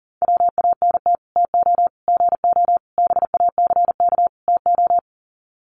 Morse code, heard as a keyed tone, about 40 wpm, 700 Hz. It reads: WANT TO GO BACK TO
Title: Sets of 5 Words 40wpm